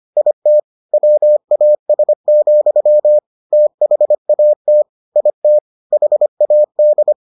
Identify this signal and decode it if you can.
Morse code, heard as a keyed tone, about 25 wpm, 600 Hz. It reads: IT WAS, THAT IT HAD